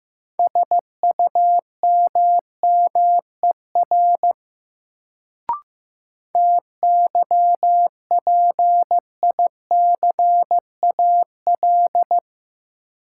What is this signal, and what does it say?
Morse code, heard as a keyed tone, about 15 wpm, 700 Hz. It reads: SUMMER TYPICAL
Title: Sentences from Top 1100 Words - New Word Drill Alt - 15wpm